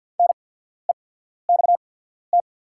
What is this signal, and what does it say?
Morse code, heard as a keyed tone, about 48 wpm, spaced as for 14 wpm, 690 Hz. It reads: NEXT
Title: 8x Character Spacing - Top 500 Words - 50wpm